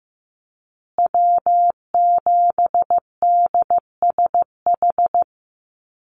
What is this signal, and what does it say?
Morse code, heard as a keyed tone, about 15 wpm, 700 Hz. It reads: W7DSH
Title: USA Call Signs Repeated 3x 15wpm